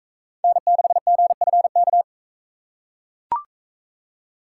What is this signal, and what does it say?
Morse code, heard as a keyed tone, about 42 wpm, 700 Hz. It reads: N6GFK